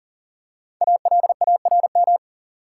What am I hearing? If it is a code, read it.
Morse code, heard as a keyed tone, about 37 wpm, 700 Hz. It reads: ALARM
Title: Five Letter Words - 40wpm